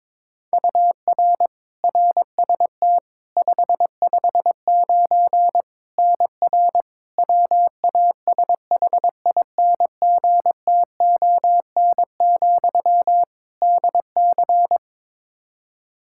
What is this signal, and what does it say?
Morse code, heard as a keyed tone, about 22 wpm, 700 Hz. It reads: UR RST 559 NR WASHINGTON, DC
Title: K1USN SST QSO Practice 22wpm